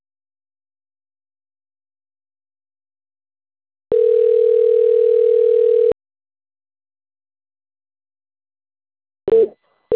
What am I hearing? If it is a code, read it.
Ringback tone (North America), 440 + 480 Hz.